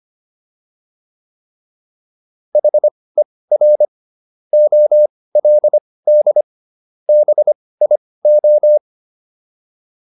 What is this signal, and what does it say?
Morse code, heard as a keyed tone, about 25 wpm, spaced as for 18 wpm, 600 Hz. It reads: HER OLD BIO